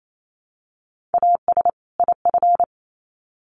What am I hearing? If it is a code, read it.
Morse code, heard as a keyed tone, about 28 wpm, 700 Hz. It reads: AS IF
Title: Sets of 2 Words 28wpm